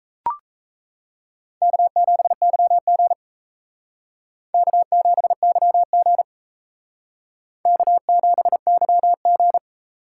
Morse code, heard as a keyed tone, about 35 wpm, 700 Hz. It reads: K7YG K7YG K7YG